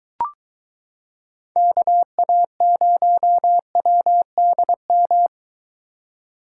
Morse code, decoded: KA0WDM

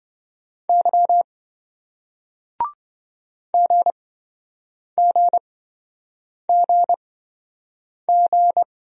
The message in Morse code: Y GGGG

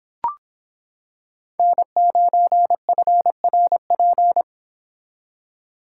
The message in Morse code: N9FRP

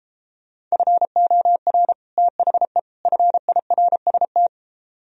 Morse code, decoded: FOR THE FIRST

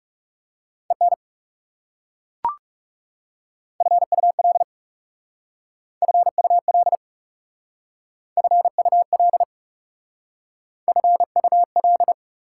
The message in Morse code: EN FUL FUL FUL FUL

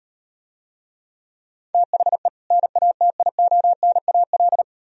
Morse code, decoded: THE NATIONAL

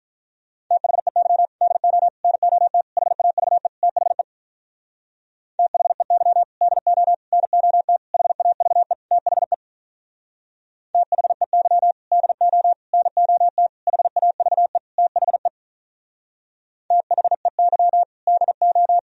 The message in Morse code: THEY DO NOT HAVE THE THEY DO NOT HAVE THE THEY DO NOT HAVE THE THEY DO

